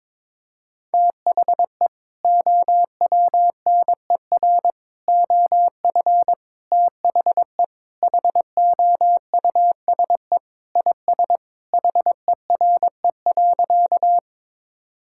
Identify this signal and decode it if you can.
Morse code, heard as a keyed tone, about 22 wpm, 700 Hz. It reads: THE OWNER OF THE HOUSE IS HERE.
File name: practice / 22wpm / Sentences with 3-Word Phrases - 22wpm